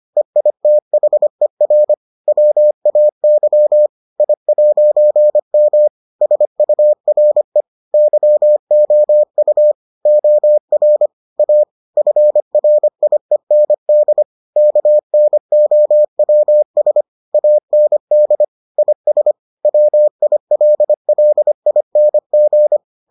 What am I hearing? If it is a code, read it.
Morse code, heard as a keyed tone, about 25 wpm, 600 Hz. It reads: EITHER WAY I'M SURE YOU OR A FRIEND KNOWS AND IS WILLING